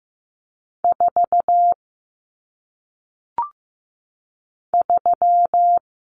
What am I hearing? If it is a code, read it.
Morse code, heard as a keyed tone, about 15 wpm, 700 Hz. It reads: 4 3
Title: Numbers - 1-Digit 15wpm